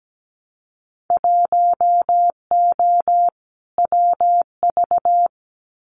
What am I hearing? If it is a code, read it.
Morse code, heard as a keyed tone, about 17 wpm, 700 Hz. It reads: 1O WV
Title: Winter Field Day Exchanges 17wpm